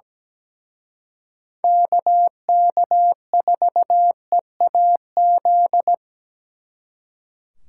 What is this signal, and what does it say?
Morse code, heard as a keyed tone, about 20 wpm, 700 Hz. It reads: KK4EAZ
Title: Speed Racing - USA Call Signs 17wpm